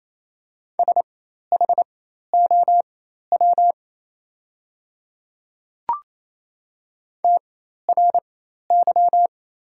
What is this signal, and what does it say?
Morse code, heard as a keyed tone, about 28 wpm, spaced as for 13 wpm, 700 Hz. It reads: SHOW TRY
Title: 4x Character Spacing - Top 200 Words - 28wpm